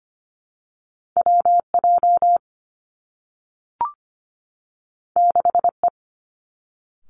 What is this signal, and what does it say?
Morse code, heard as a keyed tone, about 25 wpm, 700 Hz. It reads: W J 6 E
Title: Two Challenging Letter-Number Combinations 25wpm